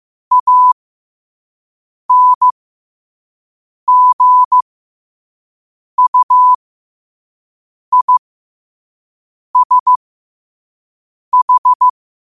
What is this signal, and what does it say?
Morse code, heard as a keyed tone, about 15 wpm, spaced as for 5 wpm, 1000 Hz. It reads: ANGUISH
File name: qotdfiles / 5wpm